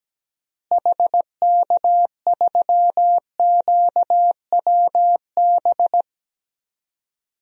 Morse code, heard as a keyed tone, about 17 wpm, 700 Hz. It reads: HK3QWB
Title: DXCC-100 Call Signs - 6 Characters - Repeated 2x - 17wpm